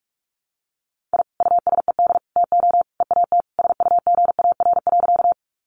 Morse code, heard as a keyed tone, about 44 wpm, 700 Hz. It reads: I USED TO EAT SUGAR.
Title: Sentences from Top 1200 Words 45wpm